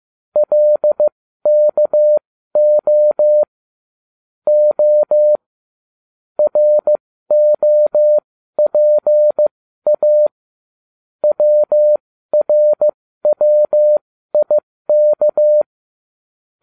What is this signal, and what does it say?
Morse code, heard as a keyed tone, about 15 wpm, spaced as for 12 wpm, 600 Hz. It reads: LKO O ROPA WRWIK